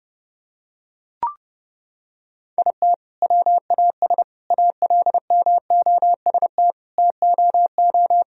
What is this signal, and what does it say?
Morse code, heard as a keyed tone, about 30 wpm, 700 Hz. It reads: IT WAS ALMOST TOO
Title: Sentences from Top 1000 Words - Review Part 4 - 30wpm